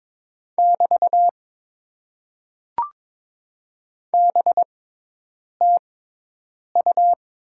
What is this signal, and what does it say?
Morse code, heard as a keyed tone, about 22 wpm, spaced as for 8 wpm, 700 Hz. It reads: <BT> BTU
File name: Top 90 QSO Elements - 6x Character Spacing - 22wpm